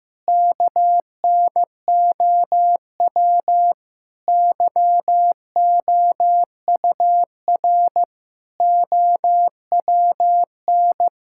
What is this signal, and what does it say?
Morse code, heard as a keyed tone, about 15 wpm, 700 Hz. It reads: KNOW YOUR OWN
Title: Sentences from Top 1000 Words - Review Part 6 - 15wpm